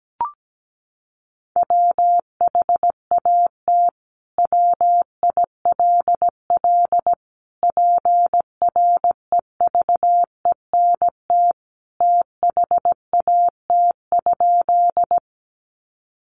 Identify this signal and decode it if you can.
Morse code, heard as a keyed tone, about 17 wpm, 700 Hz. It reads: WHAT WILL PREVENT THAT?
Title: Sentences from Top 1100 Words 17wpm